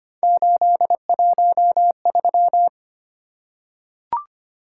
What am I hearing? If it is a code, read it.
Morse code, heard as a keyed tone, about 25 wpm, 700 Hz. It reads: 813
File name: Three Letter-Number Combinations - 25wpm